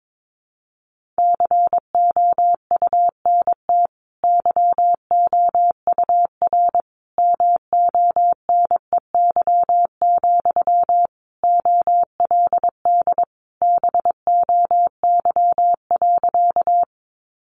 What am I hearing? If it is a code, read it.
Morse code, heard as a keyed tone, about 22 wpm, 700 Hz. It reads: COUNT YOUR MONEY, OLD BOY.
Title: Sentences from Top 1500 Words 22wpm